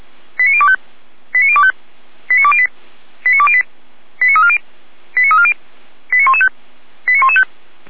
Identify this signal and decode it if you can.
Five-tone selective call (ZVEI), heard as quick runs of five tones, tones of about 70 ms. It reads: 89025 89025 89208 89208 89340 89340 89115 89115